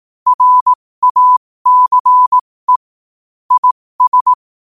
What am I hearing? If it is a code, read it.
Morse code, heard as a keyed tone, about 18 wpm, spaced as for 15 wpm, 1000 Hz. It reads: RACE IS
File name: qotdfiles / 15wpm